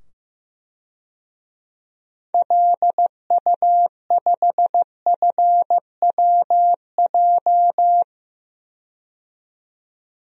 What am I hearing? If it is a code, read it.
Morse code, heard as a keyed tone, about 15 wpm, 700 Hz. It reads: LU5FWJ
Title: DXCC-100 Call Signs - 6 Characters - 15wpm